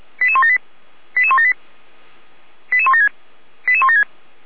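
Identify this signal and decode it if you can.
Five-tone selective call (ZVEI), heard as quick runs of five tones, tones of about 70 ms. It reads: 88178 88178 88176 88176
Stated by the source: ZVEI five-tone call 88307 received by SDR, recorded by Feuerwehr Orlamünde.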